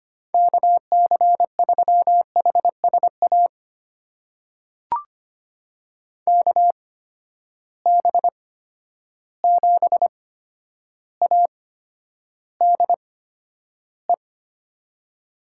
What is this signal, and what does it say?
Morse code, heard as a keyed tone, about 25 wpm, spaced as for 11 wpm, 700 Hz. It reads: KC3HSA KB7ADE
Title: USA Call Signs - 6 Characters - 8x Character Spacing - 25wpm